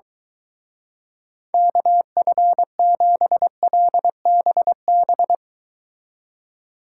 Morse code, heard as a keyed tone, about 22 wpm, 700 Hz. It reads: KF7LBB